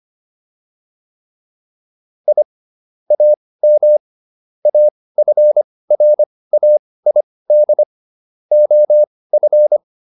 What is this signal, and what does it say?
Morse code, heard as a keyed tone, about 25 wpm, spaced as for 18 wpm, 600 Hz. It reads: I AM AFRAID OF
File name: SS108 18